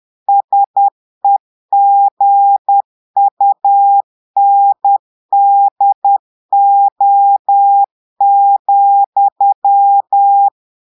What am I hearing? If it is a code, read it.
Morse code, heard as a keyed tone, about 10 wpm, 800 Hz. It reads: SEGUNDO,
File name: Portugais - texte en clair 1 - vitesse - 10 mots minute